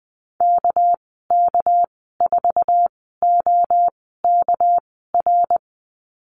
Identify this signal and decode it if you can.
Morse code, heard as a keyed tone, about 20 wpm, spaced as for 14 wpm, 700 Hz. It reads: KK4OKR
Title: 207 Lesson 33g - Letter j - Call Signs 20wpm